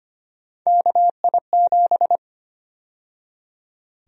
Morse code, heard as a keyed tone, about 25 wpm, 700 Hz. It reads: KI7
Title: USA Call Sign Prefixes - 25wpm